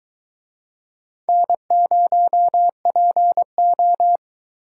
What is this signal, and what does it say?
Morse code, heard as a keyed tone, about 21 wpm, 700 Hz. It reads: N0PO